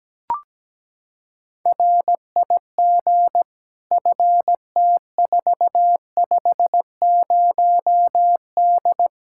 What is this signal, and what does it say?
Morse code, heard as a keyed tone, about 17 wpm, 700 Hz. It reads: RIG FT450D
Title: Practice QSO - Exchange 2 - Info - 17wpm